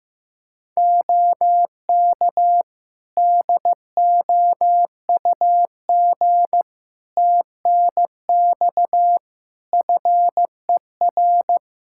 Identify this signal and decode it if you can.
Morse code, heard as a keyed tone, about 16 wpm, 700 Hz. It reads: OK DOUG TNX FER
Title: K1USN SST QSO Practice 15wpm